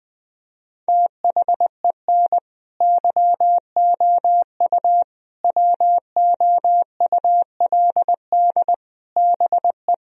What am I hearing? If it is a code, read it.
Morse code, heard as a keyed tone, about 20 wpm, 700 Hz. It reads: THEN YOU WOULD BE
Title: Sentences from Top 1000 Words - Review Part 1 - 20wpm